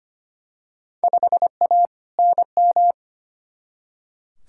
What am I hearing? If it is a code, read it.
Morse code, heard as a keyed tone, about 25 wpm, 700 Hz. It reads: 5A NM
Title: Field Day Exchanges - 25wpm